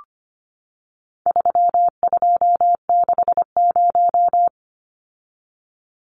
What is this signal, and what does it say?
Morse code, heard as a keyed tone, about 25 wpm, 700 Hz. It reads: 3260
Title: Numbers - 4-Digit 25wpm